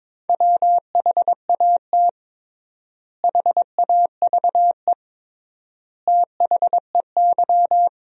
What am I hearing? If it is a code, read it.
Morse code, heard as a keyed tone, about 22 wpm, 700 Hz. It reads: WHAT HAVE THEY
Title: 3x Word-Spacing - Sentences from Top 500 Words - 22wpm